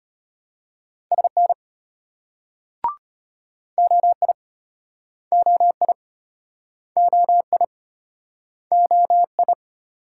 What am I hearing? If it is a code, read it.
Morse code, heard as a keyed tone, about 35 wpm, 700 Hz. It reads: SN OI OI OI OI